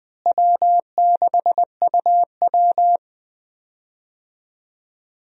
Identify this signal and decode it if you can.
Morse code, heard as a keyed tone, about 20 wpm, 700 Hz. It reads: W6UW
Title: USA Call Signs - 4 Characters - Repeated 3x - 20wpm